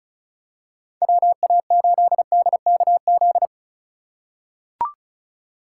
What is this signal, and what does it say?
Morse code, heard as a keyed tone, about 36 wpm, 700 Hz. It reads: WA8DKZ
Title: USA Call Signs - 8x Character Spacing - 35wpm